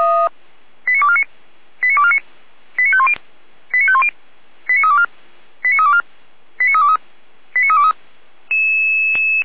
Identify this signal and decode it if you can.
Five-tone selective call (ZVEI), heard as quick runs of five tones, tones of about 70 ms. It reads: 89249 89249 89510 89510 89324 89324 89323 89323